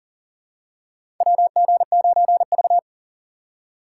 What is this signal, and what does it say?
Morse code, heard as a keyed tone, about 44 wpm, 700 Hz. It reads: WG9V
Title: USA Call Signs - 3x Character Spacing - 40wpm